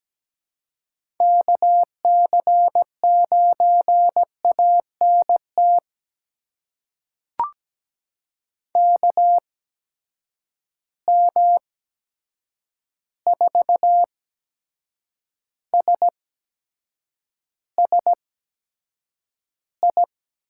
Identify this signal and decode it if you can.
Morse code, heard as a keyed tone, about 17 wpm, spaced as for 7 wpm, 700 Hz. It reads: KC9ANT KM4SSI
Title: USA Call Signs - 6 Characters - 8x Character Spacing - 17wpm